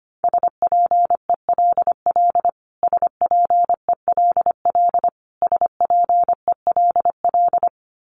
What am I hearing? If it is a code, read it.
Morse code, heard as a keyed tone, about 25 wpm, 700 Hz. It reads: SPELL SPELL SPELL
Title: Top 300 Words Repeated 25wpm